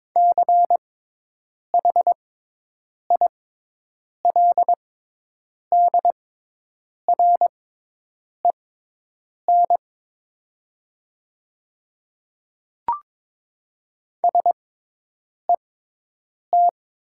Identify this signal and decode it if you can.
Morse code, heard as a keyed tone, about 22 wpm, spaced as for 8 wpm, 700 Hz. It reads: CHILDREN SET
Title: 6x Character Spacing - Top 300 Words - 22wpm